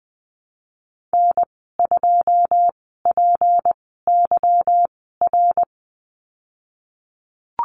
Morse code, heard as a keyed tone, about 20 wpm, spaced as for 14 wpm, 700 Hz. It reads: N2PYR
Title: 179 Lesson 29g - Slash - Call Signs 20wpm